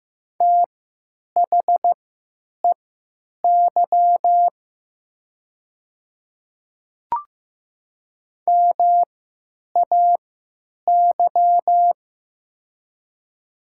Morse code, heard as a keyed tone, about 15 wpm, spaced as for 8 wpm, 700 Hz. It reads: THEY MAY
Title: 3x Character Spacing - Top 100 Words - 15wpm